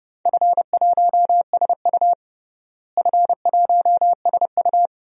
Morse code, heard as a keyed tone, about 30 wpm, 700 Hz. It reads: F1SU F1SU
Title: DXCC-100 Call Signs - 4 Characters - Repeated 2x - 30wpm